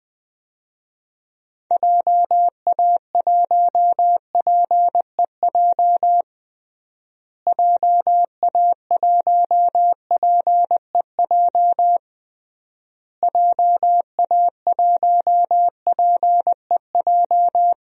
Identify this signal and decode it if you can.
Morse code, heard as a keyed tone, about 20 wpm, 700 Hz. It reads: JA1PEJ JA1PEJ JA1PEJ
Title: DXCC-100 Call Signs - 6 Characters - Repeated 3x - 20wpm